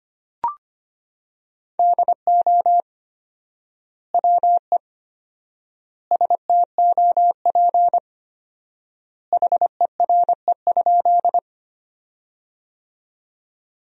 Morse code, heard as a keyed tone, about 25 wpm, 700 Hz. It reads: DO WE STOP HERE?
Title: 4x Word-Spacing - Sentences from Top 300 Words - 25wpm